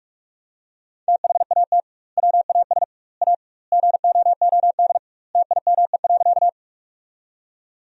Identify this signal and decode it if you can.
Morse code, heard as a keyed tone, about 46 wpm, 700 Hz. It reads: THAT WAS A GOOD TIME.
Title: 2x Word-Spacing - Sentences from Top 200 Words - 45wpm